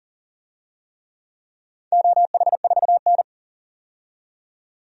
Morse code, heard as a keyed tone, about 40 wpm, 700 Hz. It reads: OH4N
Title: DXCC-100 Call Signs - 4 Characters - 40wpm